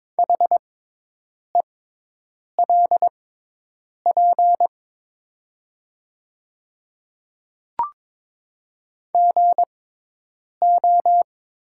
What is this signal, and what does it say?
Morse code, heard as a keyed tone, about 22 wpm, spaced as for 8 wpm, 700 Hz. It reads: HELP GO
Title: 6x Character Spacing - Top 300 Words - 22wpm